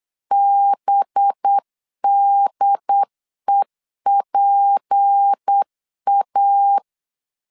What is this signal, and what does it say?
Morse code, heard as a keyed tone, about 8 wpm, 800 Hz. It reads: BDEPA
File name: CW / DET / 1 6 1500C S C 8wpm 5219